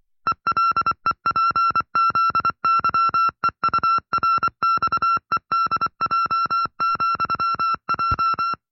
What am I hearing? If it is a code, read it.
Morse code, heard as a keyed tone, about 24 wpm, 1350 Hz. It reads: ELEPZYEURXEDJ,J
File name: FreeFiles / CW-practice / 20WPM-side1